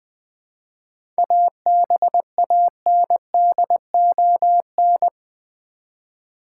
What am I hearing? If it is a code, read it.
Morse code, heard as a keyed tone, about 20 wpm, 700 Hz. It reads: ABANDON
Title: 7 Letter Words - First 3 Letters Given - 20wpm